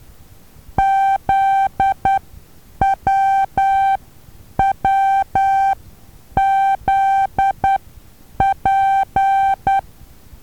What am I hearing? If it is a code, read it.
Morse code, heard as a keyed tone, about 10 wpm, spaced as for 8 wpm, 780 Hz. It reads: ZWWZP
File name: Taviro HA6VA B